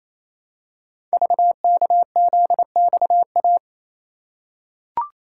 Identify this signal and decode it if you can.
Morse code, heard as a keyed tone, about 29 wpm, 700 Hz. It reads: VKZXA
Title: Code Groups - 5 Letters - 28wpm